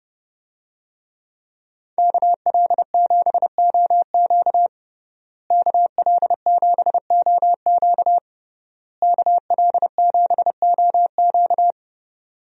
Morse code, heard as a keyed tone, about 31 wpm, 700 Hz. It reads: KL7OQ KL7OQ KL7OQ
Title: DXCC-100 Call Signs - 5 Characters - Repeated 3x - 30wpm